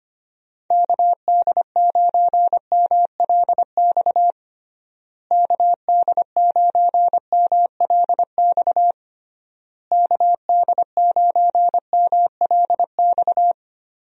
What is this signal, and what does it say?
Morse code, heard as a keyed tone, about 25 wpm, 700 Hz. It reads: KD9MLX KD9MLX KD9MLX
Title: USA Call Signs - 6 Characters - Repeated 3x - 25wpm